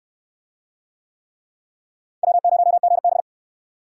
Morse code, heard as a keyed tone, about 62 wpm, 700 Hz. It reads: W0GB